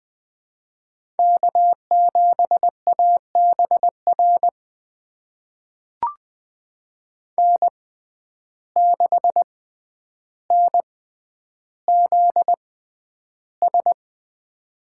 Morse code, heard as a keyed tone, about 20 wpm, spaced as for 10 wpm, 700 Hz. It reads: K7ABR N6NZS
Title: USA Call Signs - 5 Characters - 6x Character Spacing - 20wpm